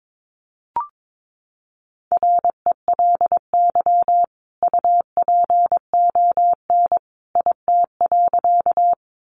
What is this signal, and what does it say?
Morse code, heard as a keyed tone, about 22 wpm, 700 Hz. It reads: RELY UPON IT.
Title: Sentences from Top 1200 Words 22wpm